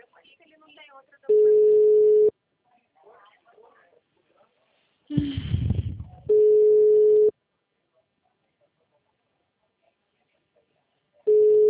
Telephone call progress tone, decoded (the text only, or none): ringback tone